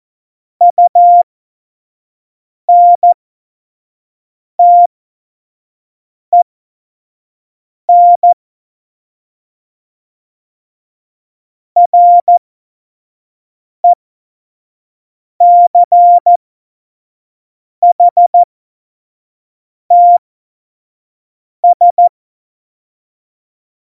Morse code, heard as a keyed tone, about 14 wpm, spaced as for 5 wpm, 700 Hz. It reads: UNTEN RECHTS
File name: Morsekurs / Witze / Witz 16. Juli 2019, 25 F70